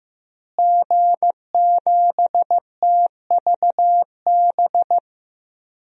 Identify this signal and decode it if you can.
Morse code, heard as a keyed tone, about 15 wpm, 700 Hz. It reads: G7TVB
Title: DXCC-100 Call Signs - 5 Characters - Repeated 3x - 15wpm